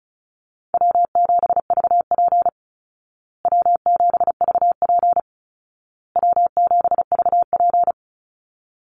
Morse code, heard as a keyed tone, about 34 wpm, 700 Hz. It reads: W7VP W7VP W7VP
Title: Easy Call Signs Repeated 3x - 35wpm